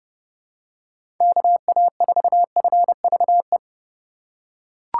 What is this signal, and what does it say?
Morse code, heard as a keyed tone, about 29 wpm, 700 Hz. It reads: KA4FVE
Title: USA Call Signs - 6 Characters - 4x Character Spacing - 30wpm